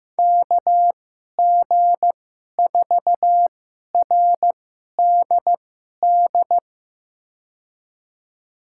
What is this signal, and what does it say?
Morse code, heard as a keyed tone, about 15 wpm, spaced as for 11 wpm, 700 Hz. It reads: KG4RDD
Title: USA Call Signs - 6 Characters - 2x Character Spacing - 15wpm